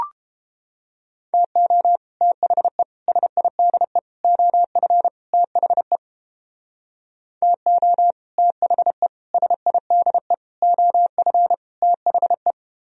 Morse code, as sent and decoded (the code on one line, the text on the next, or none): - --- / - .... / ... .. -.. / --- ..-. / - .... / - --- / - .... / ... .. -.. / --- ..-. / - .... .
TO THE SIDE OF THE TO THE SIDE OF THE